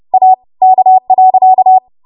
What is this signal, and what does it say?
Morse code, heard as a keyed tone, about 30 wpm, 750 Hz. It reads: A K.
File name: Morse / 30WPM